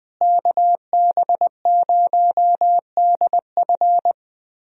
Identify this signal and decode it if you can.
Morse code, heard as a keyed tone, about 20 wpm, 700 Hz. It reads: KB0DF
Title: USA Call Signs - 5 Characters - Repeated 2x - 20wpm